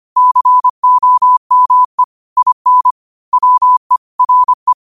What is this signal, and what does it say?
Morse code, heard as a keyed tone, about 25 wpm, 1000 Hz. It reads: COME IN WE RE